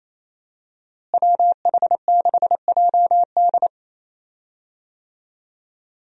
Morse code, heard as a keyed tone, about 28 wpm, 700 Hz. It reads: WH6JD
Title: USA Call Signs - 5 Characters - Repeated 3x - 28wpm